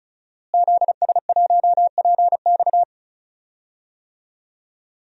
Morse code, heard as a keyed tone, about 34 wpm, 700 Hz. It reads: ZS1PX
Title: DXCC-100 Call Signs - 5 Characters - Repeated 2x - 35wpm